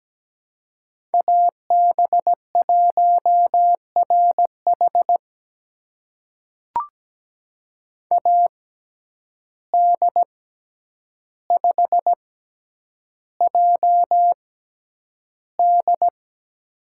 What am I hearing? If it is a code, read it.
Morse code, heard as a keyed tone, about 17 wpm, spaced as for 9 wpm, 700 Hz. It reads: AB1RH AD5JD
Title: USA Call Signs - 5 Characters - 6x Character Spacing - 17wpm